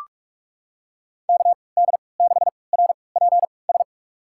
Morse code, heard as a keyed tone, about 46 wpm, spaced as for 26 wpm, 700 Hz. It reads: KD6RPS
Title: USA Call Signs - 6 Characters - 3x Character Spacing - 45wpm